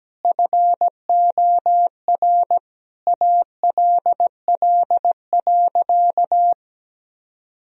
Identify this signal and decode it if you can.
Morse code, heard as a keyed tone, about 17 wpm, 700 Hz. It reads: FOR ALL.
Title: Sentences from Top 1000 Words - Review Part 4 - 17wpm